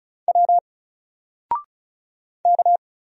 Morse code, heard as a keyed tone, about 36 wpm, 700 Hz. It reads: W K
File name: Single-Letter-Number-35wpm